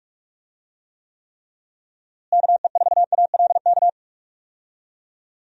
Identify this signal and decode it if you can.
Morse code, heard as a keyed tone, about 41 wpm, 700 Hz. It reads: KE4ALK
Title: USA Call Signs - 6 Characters - Repeated 3x - 45wpm